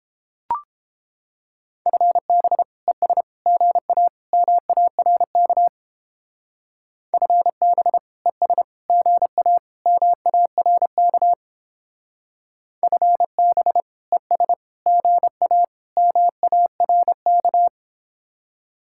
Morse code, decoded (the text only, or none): FB ES GA MARK FB ES GA MARK FB ES GA MARK